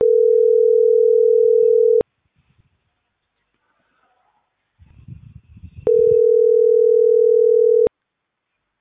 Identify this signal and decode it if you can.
Ringback tone (North America), 440 + 480 Hz.